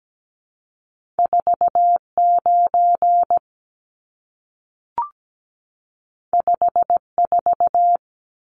Morse code, heard as a keyed tone, about 17 wpm, 700 Hz. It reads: 49 54
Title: Numbers - 2-Digit 17wpm